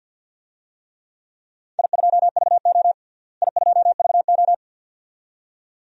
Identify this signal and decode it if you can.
Morse code, heard as a keyed tone, about 51 wpm, 700 Hz. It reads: I2VO I2VO
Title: DXCC-100 Call Signs - 4 Characters - Repeated 2x - 50wpm